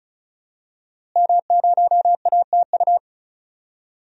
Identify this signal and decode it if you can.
Morse code, heard as a keyed tone, about 36 wpm, 700 Hz. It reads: M0ATU
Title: DXCC-100 Call Signs - 5 Characters - Repeated 3x - 35wpm